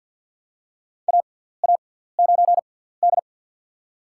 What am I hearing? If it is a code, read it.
Morse code, heard as a keyed tone, about 52 wpm, spaced as for 17 wpm, 700 Hz. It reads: AA9D
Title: USA Call Signs - 4 Characters - 6x Character Spacing - 50wpm